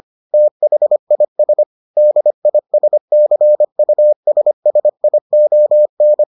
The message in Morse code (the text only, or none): THIS DISCUSSION